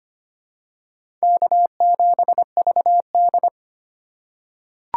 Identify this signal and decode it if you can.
Morse code, heard as a keyed tone, about 25 wpm, 700 Hz. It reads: K7VD
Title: USA Call Signs - 2x Character Spacing - 25wpm